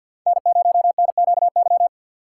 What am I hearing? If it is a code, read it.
Morse code, heard as a keyed tone, about 51 wpm, 700 Hz. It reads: N0NQY
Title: USA Call Signs - 5 Characters - 4x Character Spacing - 50wpm